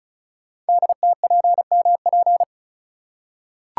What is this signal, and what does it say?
Morse code, heard as a keyed tone, about 36 wpm, 700 Hz. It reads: DTPMP
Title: Code Groups - 5 Letters - 35wpm